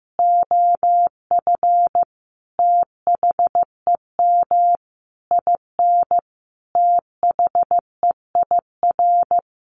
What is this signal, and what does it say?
Morse code, heard as a keyed tone, about 15 wpm, 700 Hz. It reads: OF THEM IN THEIR
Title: Sentences from Top 1100 Words 15wpm